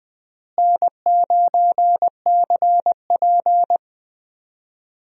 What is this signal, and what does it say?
Morse code, heard as a keyed tone, about 20 wpm, 700 Hz. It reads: N9CP
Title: USA Call Signs - 4 Characters - Repeated 2x - 20wpm